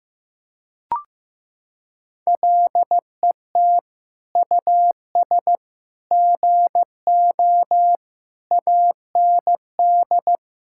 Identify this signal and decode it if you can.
Morse code, heard as a keyed tone, about 15 wpm, 700 Hz. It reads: LET US GO AND
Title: Sentences from Top 1600 Words 15wpm